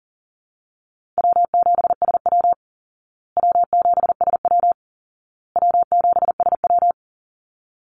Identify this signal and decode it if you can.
Morse code, heard as a keyed tone, about 40 wpm, 700 Hz. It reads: W7SW W7SW W7SW
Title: Call Signs Repeated 2x - 40wpm